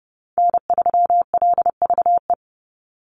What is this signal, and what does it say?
Morse code, heard as a keyed tone, about 30 wpm, 700 Hz. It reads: N3LVE
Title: USA Call Signs 30wpm